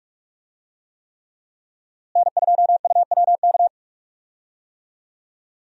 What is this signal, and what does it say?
Morse code, heard as a keyed tone, about 46 wpm, 700 Hz. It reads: N2UWK
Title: USA Call Signs - 5 Characters - Repeated 3x - 45wpm